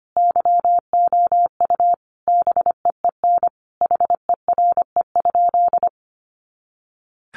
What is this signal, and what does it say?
Morse code, heard as a keyed tone, about 25 wpm, 700 Hz. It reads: YOU BEEN HERE?
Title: Sentences from Top 200 Words 25wpm